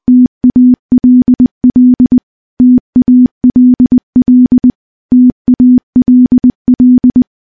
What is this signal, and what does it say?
Morse code, heard as a keyed tone, about 20 wpm, 250 Hz. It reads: TALL TALL TALL
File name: morse / morse-training-20WPM